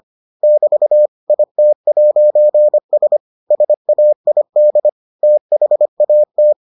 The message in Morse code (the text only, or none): <BT> IT'S SAID THAT